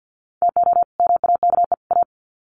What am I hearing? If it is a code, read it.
Morse code, heard as a keyed tone, about 50 wpm, 700 Hz. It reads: TO MAKE A
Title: Sets of 3 Words 50wpm